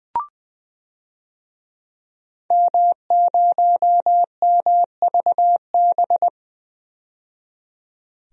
Morse code, heard as a keyed tone, about 20 wpm, 700 Hz. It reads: M0MVB
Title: DXCC-100 Call Signs - 5 Characters - 20wpm